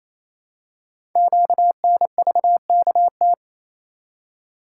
Morse code, heard as a keyed tone, about 28 wpm, 700 Hz. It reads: QNVKT
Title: Code Groups - 5 Letter:Numbers - 28wpm